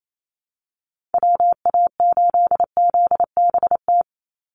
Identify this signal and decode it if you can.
Morse code, heard as a keyed tone, about 28 wpm, 700 Hz. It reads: WA8ZBT
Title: Call Signs Repeated 3x - 28wpm